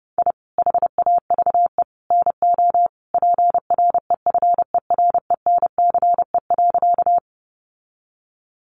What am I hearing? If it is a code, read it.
Morse code, heard as a keyed tone, about 28 wpm, 700 Hz. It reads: I HAVE NO PREFERENCE.
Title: Sentences from Top 1500 Words 30wpm